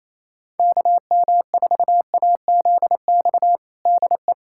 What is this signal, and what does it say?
Morse code, heard as a keyed tone, about 28 wpm, 700 Hz. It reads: KM4AZX DE
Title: USA Callsign DE Callsign - 6 Characters - 28wpm